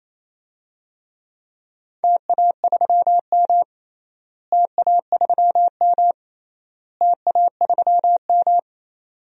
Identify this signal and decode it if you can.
Morse code, heard as a keyed tone, about 28 wpm, 700 Hz. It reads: TA3M TA3M TA3M
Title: DXCC-100 Call Signs - 4 Characters - Repeated 3x - 28wpm